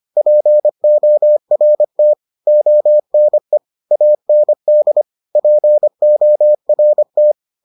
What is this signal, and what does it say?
Morse code, heard as a keyed tone, about 25 wpm, 600 Hz. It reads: PORT ONE AND PORT